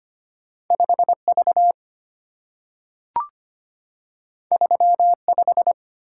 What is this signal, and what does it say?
Morse code, heard as a keyed tone, about 25 wpm, 700 Hz. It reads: 5V 35
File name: Two Letter-Number Combinations - 25wpm